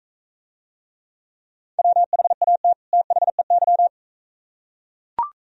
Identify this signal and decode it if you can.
Morse code, heard as a keyed tone, about 41 wpm, 700 Hz. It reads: WHAT THEY